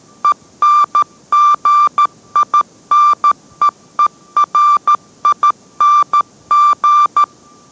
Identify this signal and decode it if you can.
Morse code, heard as a keyed tone, about 13 wpm, 1180 Hz. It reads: ENGINEERING